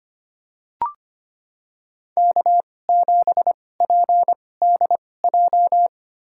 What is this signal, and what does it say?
Morse code, heard as a keyed tone, about 25 wpm, spaced as for 18 wpm, 700 Hz. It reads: K7PDJ
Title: USA Call Signs - 2x Character Spacing - 25wpm